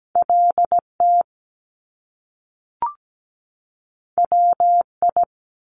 Morse code, heard as a keyed tone, about 17 wpm, 700 Hz. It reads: LT WI
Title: Top Two Letter Combinations in Words 17wpm